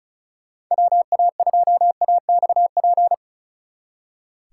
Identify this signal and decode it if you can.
Morse code, heard as a keyed tone, about 33 wpm, 700 Hz. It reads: WA2AXP
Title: Speed Racing - USA Call Signs 35wpm